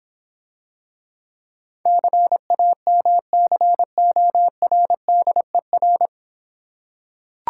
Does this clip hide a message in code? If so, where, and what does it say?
Yes: Morse code from 1.9 to 6.1 s, heard as a keyed tone, about 26 wpm, 700 Hz. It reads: CAMCORDER